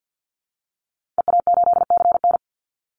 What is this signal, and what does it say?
Morse code, heard as a keyed tone, about 50 wpm, 700 Hz. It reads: EA8CN
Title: Call Signs Repeated 2x - 50wpm